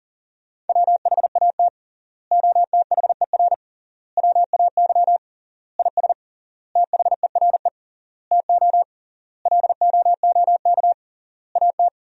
Morse code, heard as a keyed tone, about 40 wpm, 700 Hz. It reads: WHAT OTHER WAY IS THERE TO LOOK AT
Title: 3x Word-Spacing - Sentences from Top 100 Words - 40wpm